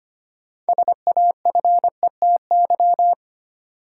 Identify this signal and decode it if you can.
Morse code, heard as a keyed tone, about 25 wpm, 700 Hz. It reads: SAFETY
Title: Sentences from Top 700 Words - New Word Drill - 25wpm